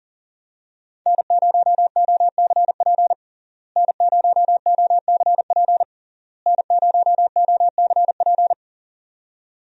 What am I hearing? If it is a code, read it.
Morse code, heard as a keyed tone, about 37 wpm, 700 Hz. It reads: N0OCP N0OCP N0OCP
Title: USA Call Signs - 5 Characters - Repeated 3x - 40wpm